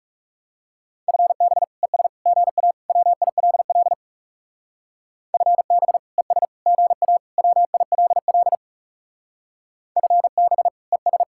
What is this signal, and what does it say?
Morse code, heard as a keyed tone, about 36 wpm, 700 Hz. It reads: FB ES GA WILL FB ES GA WILL FB ES
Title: Speed-Racing Common QSO Phrases - 30wpm